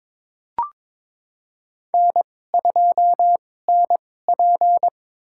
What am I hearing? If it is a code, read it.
Morse code, heard as a keyed tone, about 23 wpm, spaced as for 16 wpm, 700 Hz. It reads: N2NP
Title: USA Call Signs - 4 Characters - 2x Character Spacing - 22wpm